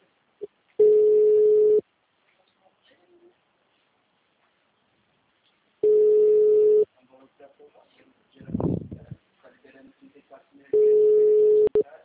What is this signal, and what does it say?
Ringback tone (Europe, 425 Hz), with speech.
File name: out- -2038- - - .35192